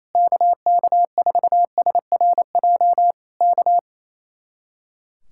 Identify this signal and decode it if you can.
Morse code, heard as a keyed tone, about 26 wpm, 700 Hz. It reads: KK4SRJ K
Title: Speed-Racing Common QSO Phrases - 28wpm